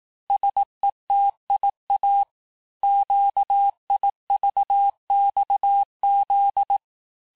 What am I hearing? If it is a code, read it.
Morse code, heard as a keyed tone, about 18 wpm, 800 Hz. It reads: SETIA QIVXZ
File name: cours-du-lundi files / cours-1588-texte-chiffre-vitesse-17MPM